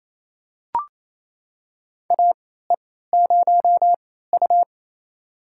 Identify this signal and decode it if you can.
Morse code, heard as a keyed tone, about 28 wpm, spaced as for 16 wpm, 700 Hz. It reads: AE0U
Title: USA Call Signs - 4 Characters - 3x Character Spacing - 28wpm